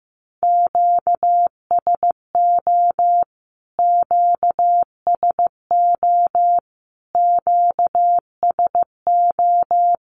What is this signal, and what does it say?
Morse code, heard as a keyed tone, about 15 wpm, 700 Hz. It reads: QSO QSO QSO
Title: Q-Codes Repeated 3x 15wpm